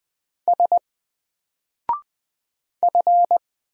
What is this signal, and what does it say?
Morse code, heard as a keyed tone, about 20 wpm, 700 Hz. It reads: S F
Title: Single Character (Letter:Number:Punctuation:Prosign) 20wpm